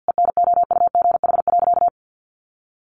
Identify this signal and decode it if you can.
Morse code, heard as a keyed tone, about 50 wpm, 700 Hz. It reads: ENOUGH.
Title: Sentences from Top 700 Words 50wpm